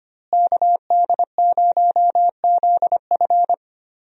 Morse code, heard as a keyed tone, about 25 wpm, 700 Hz. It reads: KD0ZF
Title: USA Call Signs - 5 Characters - Repeated 3x - 25wpm